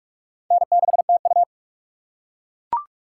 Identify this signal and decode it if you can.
Morse code, heard as a keyed tone, about 44 wpm, 700 Hz. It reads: N6TU
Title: USA Call Signs - 4 Characters - 3x Character Spacing - 45wpm